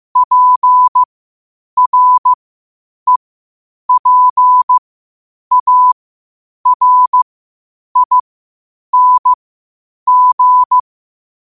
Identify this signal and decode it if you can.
Morse code, heard as a keyed tone, about 15 wpm, spaced as for 8 wpm, 1000 Hz. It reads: PREPARING
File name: qotdfiles / 8wpm